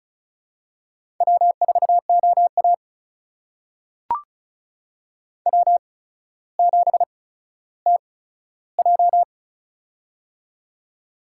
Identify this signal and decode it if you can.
Morse code, heard as a keyed tone, about 34 wpm, spaced as for 15 wpm, 700 Hz. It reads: W4OA W7TJ
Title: USA Call Signs - 4 Characters - 8x Character Spacing - 35wpm